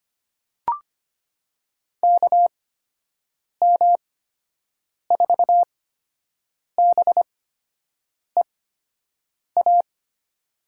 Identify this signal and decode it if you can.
Morse code, heard as a keyed tone, about 25 wpm, spaced as for 7 wpm, 700 Hz. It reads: KM4BEA
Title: USA Call Signs - 6 Characters - 8x Character Spacing - 25wpm